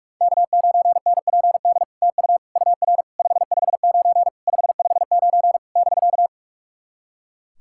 Morse code, heard as a keyed tone, about 46 wpm, 700 Hz. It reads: K9NPD TU UR 559 559 <BK>
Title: SOTA QSO Practice 45wpm